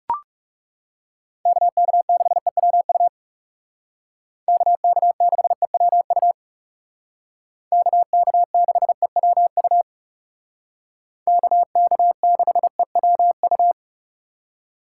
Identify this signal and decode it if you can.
Morse code, heard as a keyed tone, about 39 wpm, 700 Hz. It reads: KK6EWU KK6EWU KK6EWU KK6EWU